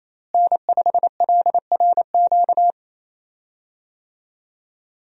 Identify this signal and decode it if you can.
Morse code, heard as a keyed tone, about 28 wpm, 700 Hz. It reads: N5LRQ
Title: USA Call Signs - 5 Characters - Repeated 2x - 28wpm